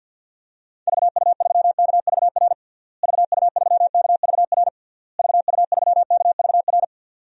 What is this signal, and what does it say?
Morse code, heard as a keyed tone, about 48 wpm, 700 Hz. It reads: VU3KVR VU3KVR VU3KVR
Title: DXCC-100 Call Signs - 6 Characters - Repeated 3x - 50wpm